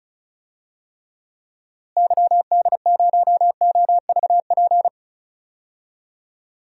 Morse code, heard as a keyed tone, about 34 wpm, 700 Hz. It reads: YD0OVP
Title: DXCC-100 Call Signs - 6 Characters - Repeated 3x - 35wpm